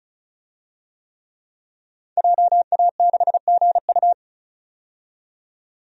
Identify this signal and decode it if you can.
Morse code, heard as a keyed tone, about 34 wpm, 700 Hz. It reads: JA6GU
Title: DXCC-100 Call Signs - 5 Characters - 35wpm